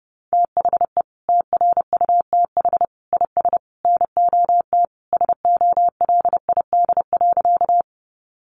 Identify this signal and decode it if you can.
Morse code, heard as a keyed tone, about 30 wpm, 700 Hz. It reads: THE TRUTH IS NOT SOLID.
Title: Sentences from Top 1500 Words 30wpm